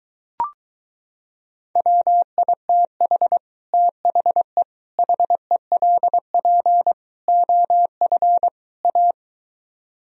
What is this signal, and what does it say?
Morse code, heard as a keyed tone, about 23 wpm, 700 Hz. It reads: WITH THE HELP OF A